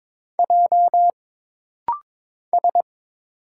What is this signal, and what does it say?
Morse code, heard as a keyed tone, about 22 wpm, 700 Hz. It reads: J S
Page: This rapid-fire timing is geared to push your ICR (Instant Character Recognition) skills.